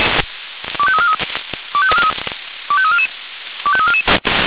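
Five-tone selective call (ZVEI), heard as quick runs of five tones, tones of about 70 ms. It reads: 25343 25344 25344
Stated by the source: ZVEI five-tone selective call received on SDR.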